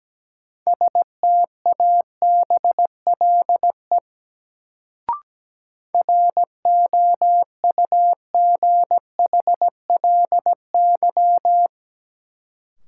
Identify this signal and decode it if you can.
Morse code, heard as a keyed tone, about 17 wpm, 700 Hz. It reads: STABLE ROUGHLY
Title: Sentences from Top 1500 Words - New Word Drill Alt - 17wpm